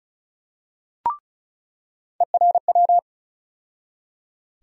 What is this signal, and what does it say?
Morse code, heard as a keyed tone, about 34 wpm, 700 Hz. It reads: ERW
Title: USA Call Sign Postfixes - 3 Letters 35wpm